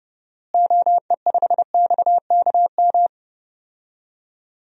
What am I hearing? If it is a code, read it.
Morse code, heard as a keyed tone, about 31 wpm, 700 Hz. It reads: OE5XKM
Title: DXCC-100 Call Signs - 6 Characters - Repeated 3x - 30wpm